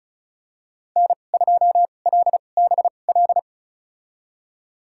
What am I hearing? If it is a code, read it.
Morse code, heard as a keyed tone, about 36 wpm, spaced as for 25 wpm, 700 Hz. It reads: N2LBL
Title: USA Call Signs - 5 Characters - 2x Character Spacing - 35wpm